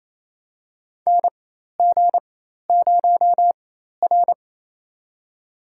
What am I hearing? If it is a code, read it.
Morse code, heard as a keyed tone, about 28 wpm, spaced as for 13 wpm, 700 Hz. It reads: NG0R
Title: USA Call Signs - 4 Characters - 4x Character Spacing - 28wpm